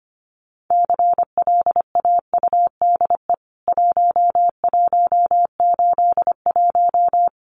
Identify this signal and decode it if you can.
Morse code, heard as a keyed tone, about 25 wpm, 700 Hz. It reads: CLAUDE 1181
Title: CWT Exchange 25wpm